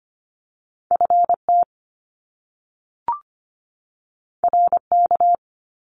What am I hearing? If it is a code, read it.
Morse code, heard as a keyed tone, about 25 wpm, 700 Hz. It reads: FT RK
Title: Top Two Letter Combinations in Words 25wpm